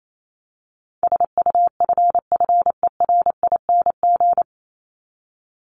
Morse code, heard as a keyed tone, about 28 wpm, 700 Hz. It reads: SUFFERING
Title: ING Suffix - Top 100 Words with Base Words 28wpm